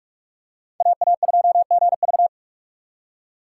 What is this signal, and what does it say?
Morse code, heard as a keyed tone, about 48 wpm, 700 Hz. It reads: AA2GV
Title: USA Call Signs - 5 Characters - 6x Character Spacing - 45wpm